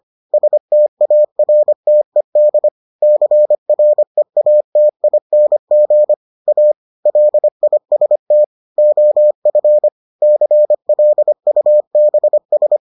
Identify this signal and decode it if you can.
Morse code, heard as a keyed tone, about 25 wpm, 600 Hz. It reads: STARTED CREATING A LIST OF CLUBS